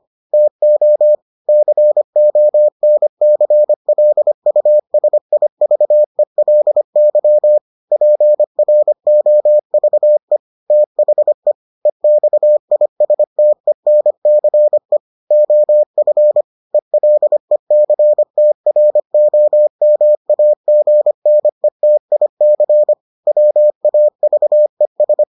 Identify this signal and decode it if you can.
Morse code, heard as a keyed tone, about 25 wpm, 600 Hz. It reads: TO CONCLUSIVELY PROVE THE EXISTENCE OF ELECTROMAGNETIC WAVES